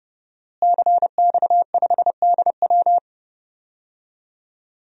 Morse code, heard as a keyed tone, about 30 wpm, 700 Hz. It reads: CX5DW
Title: DXCC-100 Call Signs - 5 Characters - Repeated 3x - 30wpm